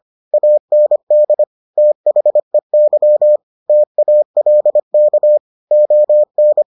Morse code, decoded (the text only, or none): AND THEY TALK ON